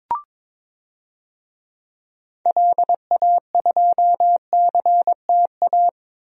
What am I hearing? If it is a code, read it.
Morse code, heard as a keyed tone, about 22 wpm, 700 Hz. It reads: LA2CTA
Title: DXCC-100 Call Signs - 6 Characters - Repeated 2x - 22wpm